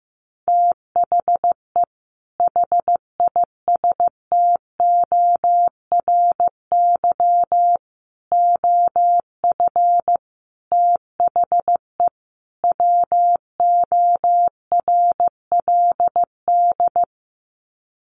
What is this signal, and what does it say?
Morse code, heard as a keyed tone, about 15 wpm, 700 Hz. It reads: THE HISTORY OF THE WORLD
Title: Sets of 5 Words Encore2 15wpm